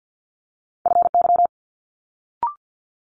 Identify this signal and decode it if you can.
Morse code, heard as a keyed tone, about 50 wpm, 700 Hz. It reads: FY
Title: Multi-letter Phonemes Quick Fire 50wpm